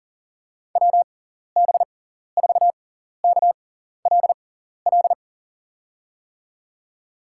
Morse code, heard as a keyed tone, about 40 wpm, spaced as for 14 wpm, 700 Hz. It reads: WB4KLL
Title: USA Call Signs - 6x Character Spacing - 40wpm